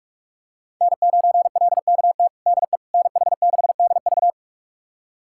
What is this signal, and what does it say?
Morse code, heard as a keyed tone, about 46 wpm, 700 Hz. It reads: N9LKT DE NH6DV
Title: USA Callsign DE Callsign - 5 Characters - 45wpm